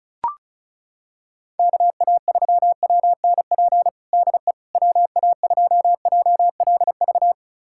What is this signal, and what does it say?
Morse code, heard as a keyed tone, about 36 wpm, 700 Hz. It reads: KA3WNP DE WA2JLV
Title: USA Callsign DE Callsign - 6 Characters - 35wpm